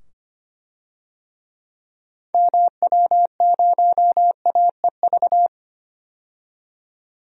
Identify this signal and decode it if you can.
Morse code, heard as a keyed tone, about 25 wpm, 700 Hz. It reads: MW0AEV
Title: DXCC-100 Call Signs - 6 Characters - 25wpm